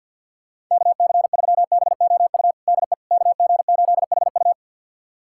Morse code, heard as a keyed tone, about 51 wpm, 700 Hz. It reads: KC3BOU DE KG8HU
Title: USA Callsign DE Callsign 50wpm